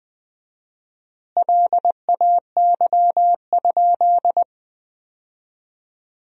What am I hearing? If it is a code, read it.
Morse code, heard as a keyed tone, about 20 wpm, 700 Hz. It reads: LAY?
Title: 4x Word-Spacing - Sentences from Top 400 Words - 20wpm